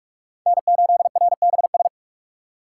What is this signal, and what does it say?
Morse code, heard as a keyed tone, about 44 wpm, 690 Hz. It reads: N8RBS
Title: USA Call Signs - 4x Character Spacing - 45wpm